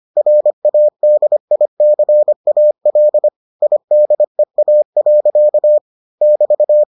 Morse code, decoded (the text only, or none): RADICAL IDEA. <BT>